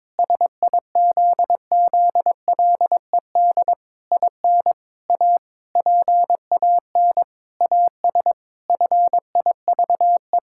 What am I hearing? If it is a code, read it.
Morse code, heard as a keyed tone, about 22 wpm, 700 Hz. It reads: SIZZLED IN A PAN AS FIVE
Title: Pangram Challenge v2 - 22wpm